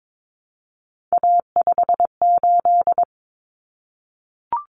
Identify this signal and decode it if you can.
Morse code, heard as a keyed tone, about 22 wpm, 700 Hz. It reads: A58
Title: Common Cut Numbers - 3-Digit 22wpm